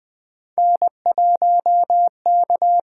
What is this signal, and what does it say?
Morse code, heard as a keyed tone, about 20 wpm, 700 Hz. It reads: N1K
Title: USA Call Signs - 5 Characters - 6x Character Spacing - 20wpm